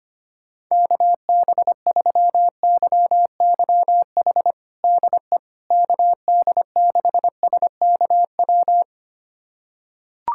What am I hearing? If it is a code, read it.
Morse code, heard as a keyed tone, about 25 wpm, 700 Hz. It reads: KB3YYH DE KD6SKW
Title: USA Callsign DE Callsign - 6 Characters - 25wpm